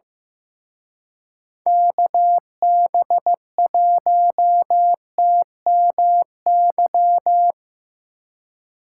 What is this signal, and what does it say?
Morse code, heard as a keyed tone, about 17 wpm, 700 Hz. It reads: KB1TMY
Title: Speed Racing - USA 2x3 Call Signs 15wpm